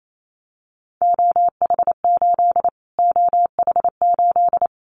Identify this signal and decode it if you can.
Morse code, heard as a keyed tone, about 28 wpm, 700 Hz. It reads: OH8 OH8
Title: Callsign Prefixes for Top 25 DXCC - Repeated 3x - 28wpm